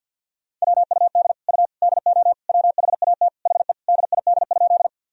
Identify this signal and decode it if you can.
Morse code, heard as a keyed tone, about 51 wpm, 700 Hz. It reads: WUD U DO WHAT HE DID?
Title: Sentences from Top 100 Words and Abbreviations 50wpm